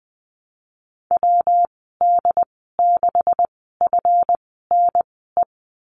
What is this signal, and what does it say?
Morse code, heard as a keyed tone, about 19 wpm, spaced as for 14 wpm, 700 Hz. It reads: WD6FNE
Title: 147 Lesson 24g - Letter y - Call Signs 20wpm